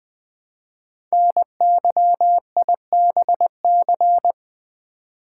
Morse code, heard as a keyed tone, about 20 wpm, 700 Hz. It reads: NYIBC
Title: Code Groups - 5 Letter:Numbers - 20wpm